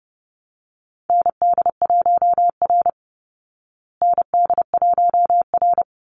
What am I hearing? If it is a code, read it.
Morse code, heard as a keyed tone, about 30 wpm, 700 Hz. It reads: ND1R ND1R
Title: Call Signs Repeated 3x - 30wpm